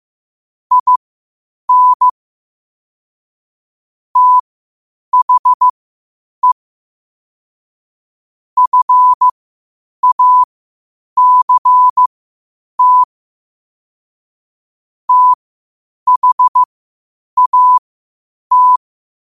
Morse code, heard as a keyed tone, about 15 wpm, spaced as for 8 wpm, 1000 Hz. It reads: IN THE FACT THAT
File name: qotdfiles / 8wpm